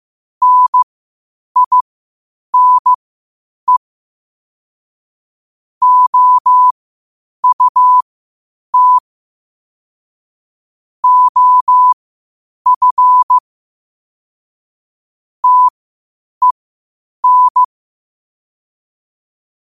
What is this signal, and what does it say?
Morse code, heard as a keyed tone, about 15 wpm, spaced as for 8 wpm, 1000 Hz. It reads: NINE OUT OF TEN
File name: qotdfiles / 8wpm